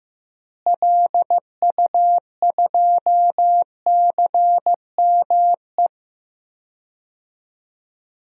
Morse code, heard as a keyed tone, about 15 wpm, 700 Hz. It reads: LU2CME
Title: DXCC-100 Call Signs - 6 Characters - Repeated 3x - 15wpm